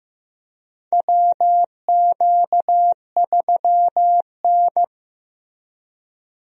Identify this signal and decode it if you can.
Morse code, heard as a keyed tone, about 15 wpm, 700 Hz. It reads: WQ3N
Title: USA Call Signs - 4 Characters - Repeated 3x - 15wpm